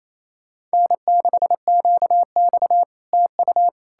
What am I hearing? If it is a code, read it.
Morse code, heard as a keyed tone, about 38 wpm, spaced as for 33 wpm, 700 Hz. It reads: N6QX TU
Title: Speed-Racing Common QSO Phrases - 28wpm